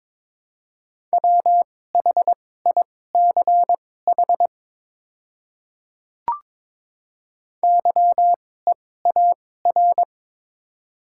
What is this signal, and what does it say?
Morse code, heard as a keyed tone, about 22 wpm, spaced as for 16 wpm, 700 Hz. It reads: WHICH YEAR
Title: 2x Character Spacing - Top 200 Words - 22wpm